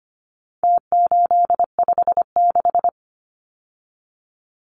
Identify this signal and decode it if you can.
Morse code, heard as a keyed tone, about 25 wpm, 700 Hz. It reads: T856
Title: Numbers - Contest Style - 4-Digit 25wpm